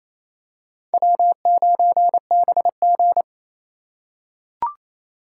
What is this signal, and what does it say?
Morse code, heard as a keyed tone, about 28 wpm, 700 Hz. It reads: W9BG
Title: USA Call Signs - 4x Character Spacing - 28wpm